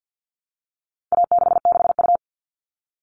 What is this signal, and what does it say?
Morse code, heard as a keyed tone, about 50 wpm, 700 Hz. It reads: A66U
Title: All Cut Numbers - 4-Digit 50wpm